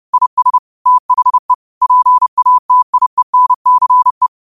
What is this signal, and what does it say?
Morse code, heard as a keyed tone, about 30 wpm, 1000 Hz. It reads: IS THE PATIENCE